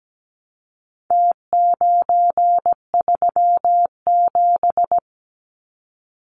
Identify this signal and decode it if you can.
Morse code, heard as a keyed tone, about 17 wpm, 700 Hz. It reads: T937
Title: Numbers - Contest Style - 4-Digit 17wpm